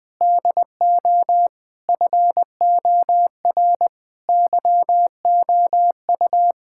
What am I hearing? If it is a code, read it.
Morse code, heard as a keyed tone, about 20 wpm, 700 Hz. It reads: DO FOR YOU
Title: Sentences with 9-Word Phrases - Spoken Beginning - 20wpm